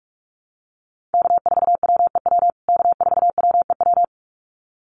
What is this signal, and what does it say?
Morse code, heard as a keyed tone, about 44 wpm, 700 Hz. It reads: K4WEW K4WEW
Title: USA Call Signs Repeated 2x 45wpm